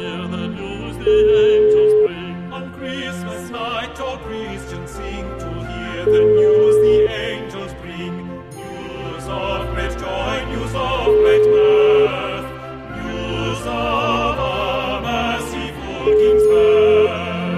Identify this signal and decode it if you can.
Ringback tone (Europe, 425 Hz), with noise.